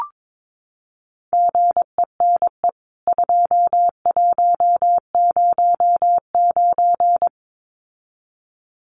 Morse code, heard as a keyed tone, about 22 wpm, 700 Hz. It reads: GENE 2109
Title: CWT Exchange 22wpm